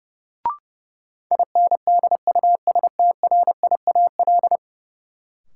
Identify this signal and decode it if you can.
Morse code, heard as a keyed tone, about 32 wpm, 700 Hz. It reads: INDUSTRIAL
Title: Sets of 5 Words - New Word Drill - 30wpm